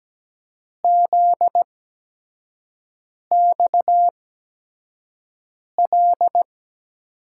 Morse code, heard as a keyed tone, about 17 wpm, spaced as for 5 wpm, 700 Hz. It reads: ZXL
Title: USA Call Signs - 6 Characters - 8x Character Spacing - 17wpm